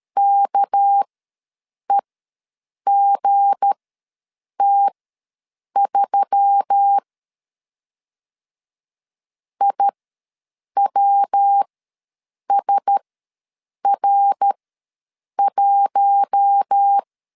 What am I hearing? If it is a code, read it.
Morse code, heard as a keyed tone, about 13 wpm, spaced as for 7 wpm, 800 Hz. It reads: KEGT3 IWSR1
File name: LEZIONI / 7 GF9Z / 7 GF9 ws 12wpm 3344